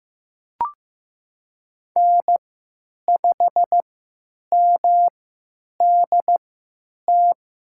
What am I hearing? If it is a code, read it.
Morse code, heard as a keyed tone, about 15 wpm, spaced as for 9 wpm, 700 Hz. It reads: N5MDT
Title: USA Call Signs - 5 Characters - 3x Character Spacing - 15wpm